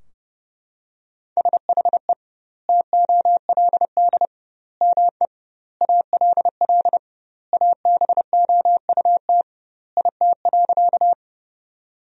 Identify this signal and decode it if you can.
Morse code, heard as a keyed tone, about 30 wpm, 700 Hz. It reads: SHE TOLD ME ALL ABOUT IT.
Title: 2x Word-Spacing - Sentences from Top 400 Words - 30wpm